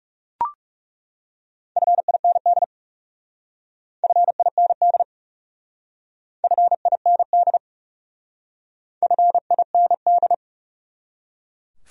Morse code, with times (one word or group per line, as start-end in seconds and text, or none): 1.8-2.6 FIND
4.0-5.0 FIND
6.4-7.6 FIND
9.0-10.3 FIND